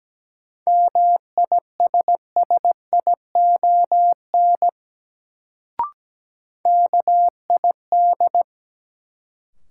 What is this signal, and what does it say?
Morse code, heard as a keyed tone, about 17 wpm, 700 Hz. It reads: MISSION KID
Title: Sentences from Top 1600 Words - New Word Drill Alt - 17wpm